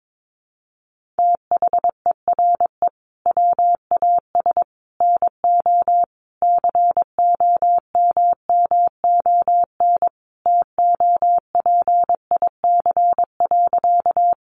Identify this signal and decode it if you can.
Morse code, heard as a keyed tone, about 22 wpm, 700 Hz. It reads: THERE WAS NO COMMON TOPIC.
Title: Sentences from Top 1200 Words 22wpm